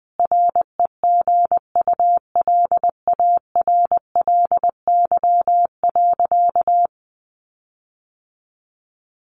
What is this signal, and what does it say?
Morse code, heard as a keyed tone, about 19 wpm, 700 Hz. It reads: REGULARLY.
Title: Sentences from Top 1400 Words 20wpm